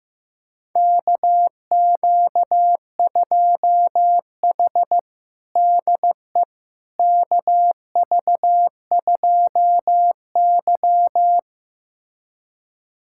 Morse code, heard as a keyed tone, about 15 wpm, 700 Hz. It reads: KQ2H DE KV2Y
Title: USA Callsign DE Callsign - 4 Characters - 15wpm